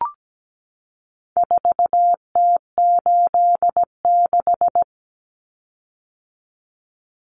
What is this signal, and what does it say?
Morse code, heard as a keyed tone, about 17 wpm, 700 Hz. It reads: 4T86
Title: Common Cut Numbers - 4-Digit 17wpm